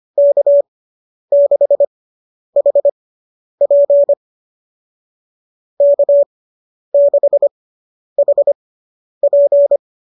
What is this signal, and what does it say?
Morse code, heard as a keyed tone, about 25 wpm, spaced as for 10 wpm, 590 Hz. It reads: K6HP K6HP